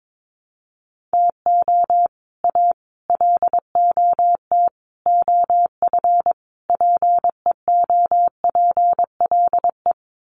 Morse code, heard as a keyed tone, about 22 wpm, 700 Hz. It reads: TO A LOT OF PEOPLE
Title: Sets of 5 Words Encore 22wpm